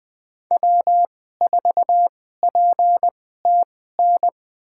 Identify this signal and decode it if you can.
Morse code, heard as a keyed tone, about 20 wpm, spaced as for 14 wpm, 700 Hz. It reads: W4PTN
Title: USA Call Signs - 5 Characters - 2x Character Spacing - 20wpm